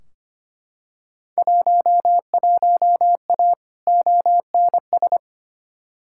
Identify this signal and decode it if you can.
Morse code, heard as a keyed tone, about 25 wpm, 700 Hz. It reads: 11A ONS
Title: Field Day Exchanges Encore - 25wpm